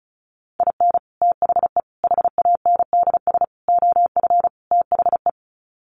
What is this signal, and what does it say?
Morse code, heard as a keyed tone, about 34 wpm, 700 Hz. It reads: IN THE HANDS OF THE
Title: Sets of 5 Words 35wpm